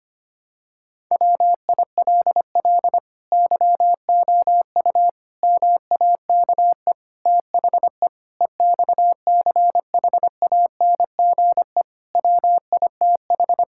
Morse code, heard as a keyed tone, about 25 wpm, 700 Hz. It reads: WILL YOU MAKE THE EXCHANGE WITH
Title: Sentences from Top 1000 Words - Review Part 6 - 25wpm